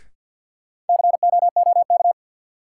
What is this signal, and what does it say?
Morse code, heard as a keyed tone, about 52 wpm, 700 Hz. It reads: COOK
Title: Sentences from Top 700 Words - New Word Drill - 50wpm